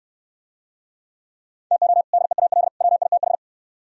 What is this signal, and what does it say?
Morse code, heard as a keyed tone, about 67 wpm, 700 Hz. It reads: TO DEAL WITH